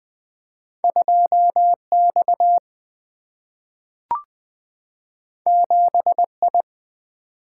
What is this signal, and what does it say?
Morse code, heard as a keyed tone, about 20 wpm, 700 Hz. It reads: 2X 7I
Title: Single-Letter Single-Number 20wpm